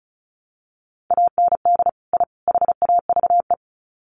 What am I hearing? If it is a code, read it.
Morse code, heard as a keyed tone, about 34 wpm, 700 Hz. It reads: AND I HAVE
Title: Sets of 3 Words 35wpm